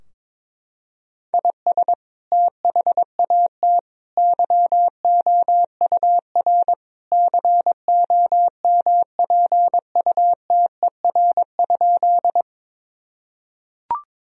Morse code, heard as a keyed tone, about 22 wpm, 700 Hz. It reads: IS THAT YOUR COMPUTER?
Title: Sentences from Top 1000 Words - Review Part 3 - 22wpm